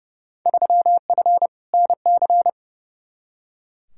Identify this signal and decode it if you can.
Morse code, heard as a keyed tone, about 30 wpm, 700 Hz. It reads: 3F NC
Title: Field Day Exchanges Encore - 30wpm